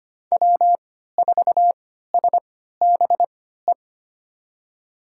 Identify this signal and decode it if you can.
Morse code, heard as a keyed tone, about 25 wpm, spaced as for 14 wpm, 700 Hz. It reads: W4SBE
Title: USA Call Signs - 5 Characters - 3x Character Spacing - 25wpm